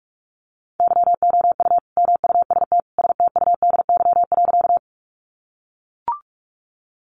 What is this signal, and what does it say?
Morse code, heard as a keyed tone, about 44 wpm, 700 Hz. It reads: YOU MUST STUDY.
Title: Sentences from Top 300 Words 45wpm